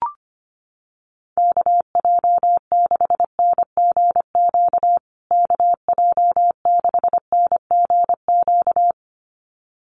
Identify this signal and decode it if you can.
Morse code, heard as a keyed tone, about 25 wpm, 700 Hz. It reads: KJ6NGQ KJ6NGQ
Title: USA Call Signs Repeated 2x 25wpm